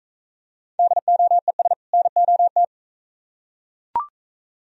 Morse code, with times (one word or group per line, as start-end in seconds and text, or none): 0.8-1.7 DOES
1.9-2.6 NOT